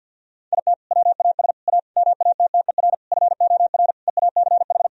Morse code, heard as a keyed tone, about 50 wpm, 700 Hz. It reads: IT WAS A MATTER FOR EACH
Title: Sentences from Top 1000 Words - Review Part 4 - 50wpm